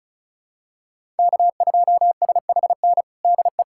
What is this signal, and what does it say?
Morse code, heard as a keyed tone, about 34 wpm, 700 Hz. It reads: K2SHN DE
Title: USA Callsign DE Callsign - 5 Characters - 35wpm